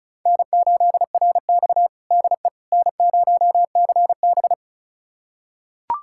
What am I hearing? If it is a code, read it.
Morse code, heard as a keyed tone, about 36 wpm, 700 Hz. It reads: N8RX DE N0CB
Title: USA Callsign DE Callsign - 4 Characters - 35wpm